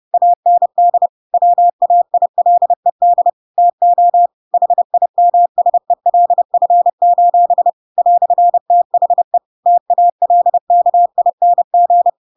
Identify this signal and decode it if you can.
Morse code, heard as a keyed tone, about 30 wpm, 700 Hz. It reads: AND WAILED TO HIMSELF: "THE TALKING
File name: Pinocchio / Pinocchio ch5 30 30 ik1hge